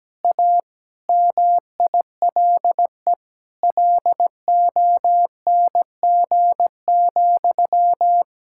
Morse code, heard as a keyed tone, about 17 wpm, 700 Hz. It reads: A MILE LONG,
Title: Sentences from Top 1600 Words 17wpm